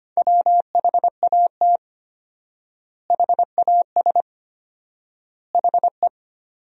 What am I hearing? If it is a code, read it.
Morse code, heard as a keyed tone, about 25 wpm, 700 Hz. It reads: WHAT HAS HE
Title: 4x Word-Spacing - Sentences from Top 500 Words - 25wpm